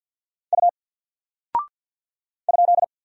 Morse code, heard as a keyed tone, about 50 wpm, 700 Hz. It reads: U ?
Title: Single Character (Letter:Number:Punctuation:Prosign) 50wpm